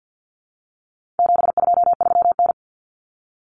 Morse code, heard as a keyed tone, about 50 wpm, 700 Hz. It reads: 723N
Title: Common Cut Numbers - 4-Digit 50wpm